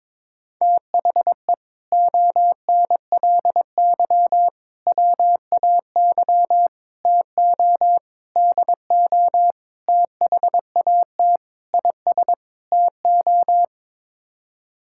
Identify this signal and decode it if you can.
Morse code, heard as a keyed tone, about 22 wpm, 700 Hz. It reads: THE ONLY WAY TO DO THAT IS TO
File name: practice / 22wpm / Sets of 8 Words - 22wpm